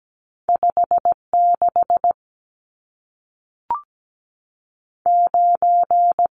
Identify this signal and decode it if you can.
Morse code, heard as a keyed tone, about 17 wpm, spaced as for 5 wpm, 700 Hz. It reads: 569
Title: Numbers - 2-Digit 17wpm